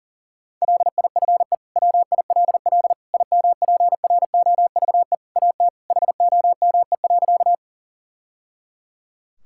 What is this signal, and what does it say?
Morse code, heard as a keyed tone, about 40 wpm, 690 Hz. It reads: LIFE WILL IMPROVE AT HOME.
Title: Sentences from Top 1000 Words - Review Part 5 - 40wpm